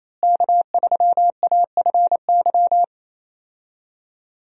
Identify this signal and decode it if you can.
Morse code, heard as a keyed tone, about 28 wpm, 700 Hz. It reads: K3AFY
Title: USA Call Signs - 5 Characters - Repeated 2x - 28wpm